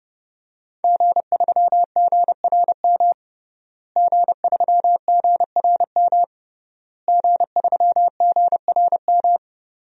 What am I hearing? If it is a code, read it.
Morse code, heard as a keyed tone, about 30 wpm, 700 Hz. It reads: G3GRM G3GRM G3GRM
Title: DXCC-100 Call Signs - 5 Characters - Repeated 3x - 30wpm